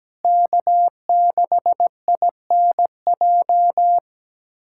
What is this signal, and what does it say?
Morse code, heard as a keyed tone, about 17 wpm, 700 Hz. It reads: K6INJ
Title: USA Callsign DE Callsign - 5 Characters - 17wpm